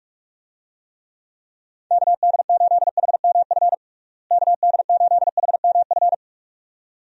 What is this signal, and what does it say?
Morse code, heard as a keyed tone, about 46 wpm, 700 Hz. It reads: KD8HMF KD8HMF
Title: USA Call Signs - 6 Characters - Repeated 2x - 45wpm